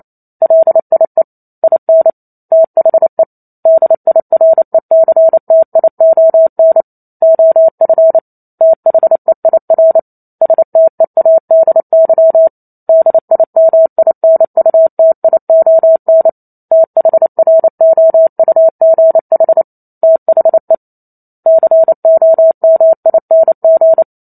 Morse code, heard as a keyed tone, about 28 wpm, 650 Hz. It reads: LIE IN THE DIRECTION OF THEIR STEADY DIMINUTION THROUGH THE COMING
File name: Chapter 2 - 28 WPM